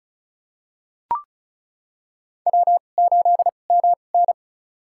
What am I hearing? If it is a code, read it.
Morse code, heard as a keyed tone, about 36 wpm, spaced as for 25 wpm, 700 Hz. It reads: W8MN
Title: USA Call Signs - 4 Characters - 2x Character Spacing - 35wpm